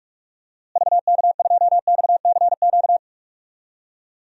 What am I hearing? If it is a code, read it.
Morse code, heard as a keyed tone, about 46 wpm, 700 Hz. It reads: VK2XCQ
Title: DXCC-100 Call Signs - 6 Characters - Repeated 3x - 45wpm